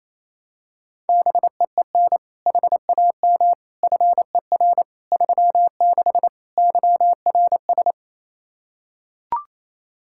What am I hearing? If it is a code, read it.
Morse code, heard as a keyed tone, about 28 wpm, 700 Hz. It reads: BEEN HAM FER 36 YRS
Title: Practice QSO - Exchange 3 - More Info - 28wpm